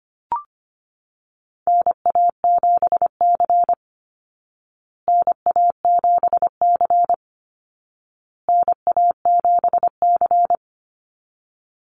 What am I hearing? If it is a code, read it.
Morse code, heard as a keyed tone, about 25 wpm, 700 Hz. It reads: NA7C NA7C NA7C
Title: Call Signs Repeated 2x - 25wpm